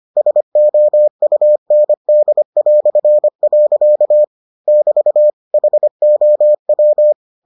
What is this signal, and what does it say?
Morse code, heard as a keyed tone, about 25 wpm, 600 Hz. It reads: SOUND". <BT> HOW